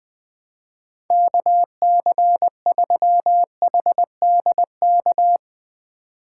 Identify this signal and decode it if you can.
Morse code, heard as a keyed tone, about 21 wpm, 700 Hz. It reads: KC3HDK
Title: USA Call Signs - 6 Characters - 4x Character Spacing - 20wpm